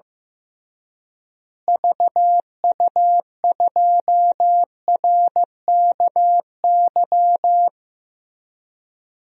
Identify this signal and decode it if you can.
Morse code, heard as a keyed tone, about 15 wpm, 700 Hz. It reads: VU2RKY
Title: DXCC-100 Call Signs - 6 Characters - Repeated 3x - 15wpm